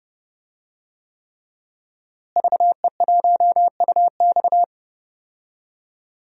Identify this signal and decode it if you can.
Morse code, heard as a keyed tone, about 30 wpm, 700 Hz. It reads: VE1UX
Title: DXCC-100 Call Signs - 5 Characters - Repeated 3x - 30wpm